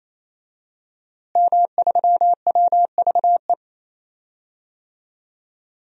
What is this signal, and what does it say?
Morse code, heard as a keyed tone, about 27 wpm, 700 Hz. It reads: M3WVE
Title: DXCC-100 Call Signs - 5 Characters - Repeated 3x - 28wpm